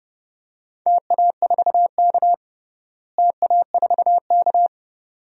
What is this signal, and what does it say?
Morse code, heard as a keyed tone, about 30 wpm, 700 Hz. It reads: TA4K TA4K
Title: DXCC-100 Call Signs - 4 Characters - Repeated 2x - 30wpm